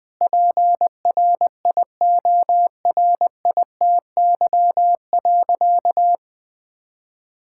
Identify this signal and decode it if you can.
Morse code, heard as a keyed tone, about 20 wpm, 700 Hz. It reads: PRIORITY.
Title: Sentences from Top 1600 Words 20wpm